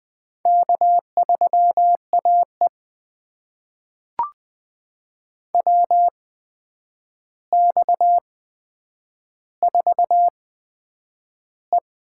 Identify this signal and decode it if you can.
Morse code, heard as a keyed tone, about 20 wpm, spaced as for 9 wpm, 700 Hz. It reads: K3AE WX4E
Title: USA Call Signs - 4 Characters - 8x Character Spacing - 20wpm